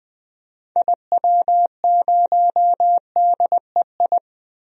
Morse code, heard as a keyed tone, about 20 wpm, 700 Hz. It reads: IW0DEI
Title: DXCC-100 Call Signs - 6 Characters - 20wpm